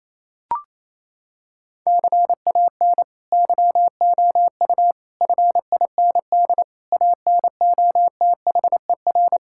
Morse code, heard as a keyed tone, about 28 wpm, 700 Hz. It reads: CAN YOU FIND ANOTHER
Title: Sentences from Top 1600 Words 28wpm